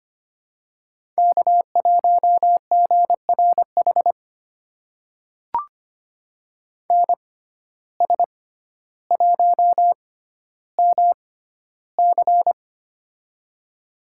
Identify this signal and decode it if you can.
Morse code, heard as a keyed tone, about 25 wpm, spaced as for 13 wpm, 700 Hz. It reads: K1GRH NS1MC
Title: USA Call Signs - 5 Characters - 6x Character Spacing - 25wpm